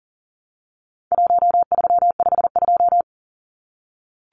Morse code, heard as a keyed tone, about 40 wpm, 700 Hz. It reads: 1352
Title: Numbers - Contest Style - 4-Digit 40wpm